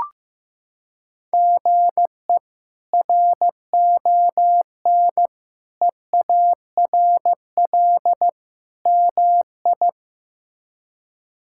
Morse code, heard as a keyed tone, about 15 wpm, 700 Hz. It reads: GE RON EARL MI
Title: K1USN SST QSO Practice - 15wpm